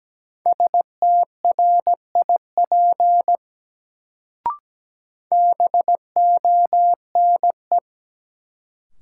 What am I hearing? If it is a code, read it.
Morse code, heard as a keyed tone, about 17 wpm, 700 Hz. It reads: STRIP BONE
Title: Sentences from Top 1600 Words - New Word Drill - 17wpm